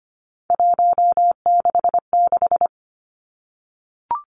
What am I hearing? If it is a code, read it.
Morse code, heard as a keyed tone, about 25 wpm, 700 Hz. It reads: 166
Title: Three Challenging Letter-Number Combinations 25wpm